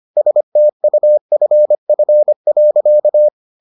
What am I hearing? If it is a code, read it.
Morse code, heard as a keyed tone, about 25 wpm, 600 Hz. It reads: STUFF.